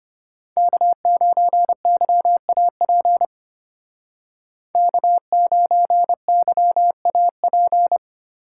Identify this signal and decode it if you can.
Morse code, heard as a keyed tone, about 29 wpm, 700 Hz. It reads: K9YAP K9YAP
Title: Speed Racing - USA 1x3 Call Signs 25wpm